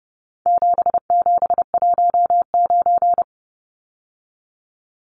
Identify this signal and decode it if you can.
Morse code, heard as a keyed tone, about 30 wpm, 700 Hz. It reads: 7719
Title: Numbers - 4-Digit 30wpm